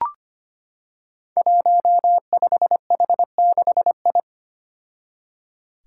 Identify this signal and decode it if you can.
Morse code, heard as a keyed tone, about 25 wpm, 700 Hz. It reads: 15H6I
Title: Five Challenging Letter-Number Combinations 25wpm